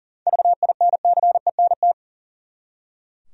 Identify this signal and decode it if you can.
Morse code, heard as a keyed tone, about 37 wpm, 700 Hz. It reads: VINCENT
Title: USA Common Names - 40wpm